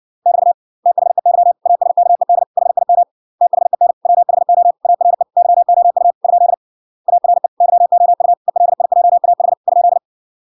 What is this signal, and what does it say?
Morse code, heard as a keyed tone, about 56 wpm, 700 Hz. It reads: <BT> THEY ASKED HIM THEN WHO ARE YOU ? ARE YOU ELIJAH ?